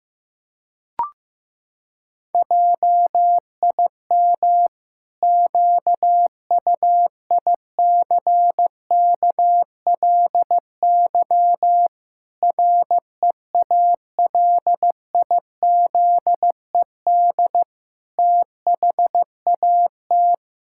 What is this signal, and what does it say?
Morse code, heard as a keyed tone, about 15 wpm, 700 Hz. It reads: JIM QUICKLY REALIZED THAT
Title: Pangram Challenge v2 - 15wpm